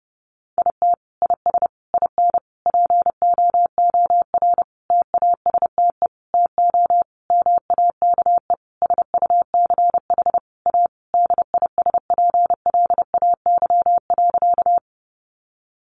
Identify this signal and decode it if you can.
Morse code, heard as a keyed tone, about 30 wpm, 700 Hz. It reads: IT IS IN POOR TASTE TO MAKE SUCH A DISPLAY.
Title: Sentences from Top 1200 Words 30wpm